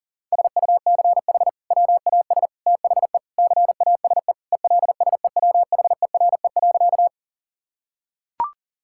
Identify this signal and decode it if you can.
Morse code, heard as a keyed tone, about 37 wpm, 700 Hz. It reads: SUCH WAS THE CASE ELSEWHERE.
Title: Sentences from Top 1600 Words 40wpm